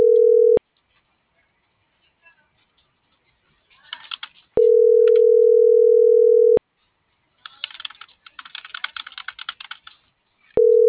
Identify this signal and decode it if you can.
Ringback tone (North America), 440 + 480 Hz.